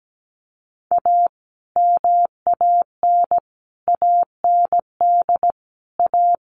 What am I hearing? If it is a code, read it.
Morse code, heard as a keyed tone, about 17 wpm, 700 Hz. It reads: A MAN AND A
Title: Sets of 5 Words Encore 17wpm